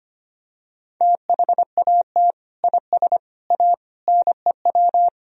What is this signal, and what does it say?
Morse code, heard as a keyed tone, about 25 wpm, 700 Hz. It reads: THAT IS A NEW
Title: Sentences from Top 1000 Words - Review Part 3 - 25wpm